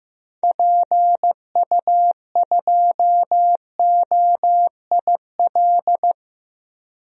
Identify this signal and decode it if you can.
Morse code, heard as a keyed tone, about 15 wpm, 700 Hz. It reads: PU2OIL
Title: DXCC-100 Call Signs - 6 Characters - 15wpm